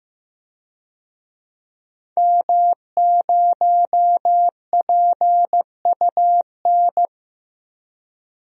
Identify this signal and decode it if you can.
Morse code, heard as a keyed tone, about 15 wpm, 700 Hz. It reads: M0PUN
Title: DXCC-100 Call Signs - 5 Characters - Repeated 2x - 15wpm